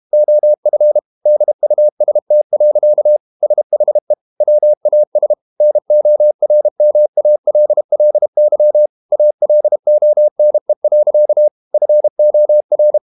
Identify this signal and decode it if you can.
Morse code, heard as a keyed tone, about 32 wpm, 600 Hz. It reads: OF DUST. SHE WAS NORMALLY ALONE. FOR
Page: Never 31wpm | CW med Gnister